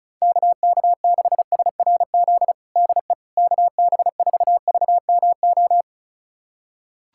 Morse code, heard as a keyed tone, about 34 wpm, 700 Hz. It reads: KK6SRZ DE KB4VMO
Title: USA Callsign DE Callsign - 6 Characters - 35wpm